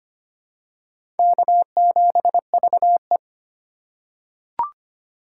Morse code, heard as a keyed tone, about 25 wpm, 700 Hz. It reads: K7VE
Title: USA Call Signs - 4x Character Spacing - 25wpm